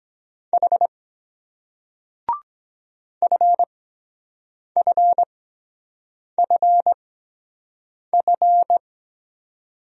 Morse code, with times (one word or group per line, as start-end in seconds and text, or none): 0.5-0.9 H
3.2-8.8 FFFF